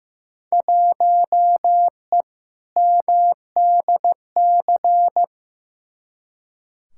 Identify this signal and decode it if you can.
Morse code, heard as a keyed tone, about 15 wpm, 700 Hz. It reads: 1E MDC
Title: Field Day Exchanges - 15wpm